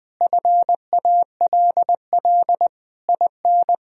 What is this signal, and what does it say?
Morse code, heard as a keyed tone, about 20 wpm, 700 Hz. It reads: FALL IN
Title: Sentences from Top 1000 Words - Review Part 4 - 20wpm